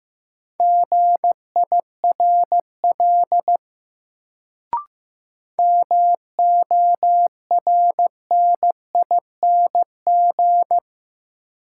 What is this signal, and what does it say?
Morse code, heard as a keyed tone, about 15 wpm, 700 Hz. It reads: GIRL MORNING
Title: Sentences from Top 400 Words - New Word Drill - 15wpm